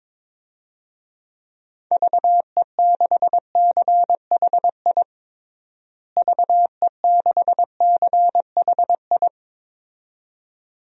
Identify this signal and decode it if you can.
Morse code, heard as a keyed tone, about 22 wpm, 700 Hz. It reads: VE6CHI VE6CHI
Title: DXCC-100 Call Signs - 6 Characters - Repeated 2x - 22wpm